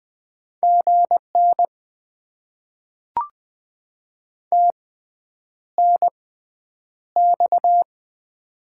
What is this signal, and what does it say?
Morse code, heard as a keyed tone, about 20 wpm, spaced as for 9 wpm, 700 Hz. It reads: GN TNX
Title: Top 90 QSO Elements - 6x Character Spacing - 20wpm